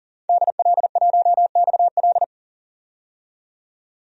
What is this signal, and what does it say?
Morse code, heard as a keyed tone, about 40 wpm, 700 Hz. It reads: DL1XL
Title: DXCC-100 Call Signs - 5 Characters - 40wpm